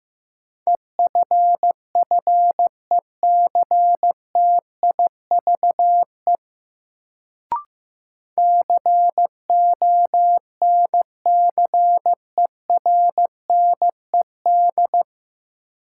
Morse code, heard as a keyed tone, about 15 wpm, 700 Hz. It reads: EFFECTIVE CONCERNED
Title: Sentences from Top 1000 Words - Review Part 6 - New Word Drill - 15wpm